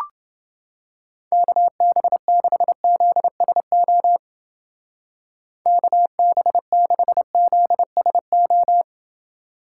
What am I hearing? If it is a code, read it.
Morse code, heard as a keyed tone, about 29 wpm, 700 Hz. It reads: KB6ZSO KB6ZSO